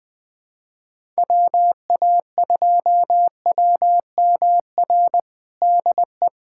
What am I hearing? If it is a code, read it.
Morse code, heard as a keyed tone, about 20 wpm, 700 Hz. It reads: WA2WMR DE
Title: USA Callsign DE Callsign - 6 Characters - 20wpm